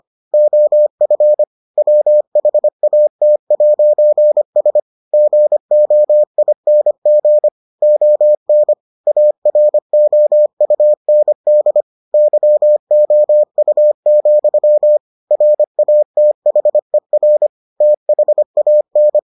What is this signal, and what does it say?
Morse code, heard as a keyed tone, about 25 wpm, 600 Hz. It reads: OF WHAT'S GOING ON AROUND YOU, RATHER THAN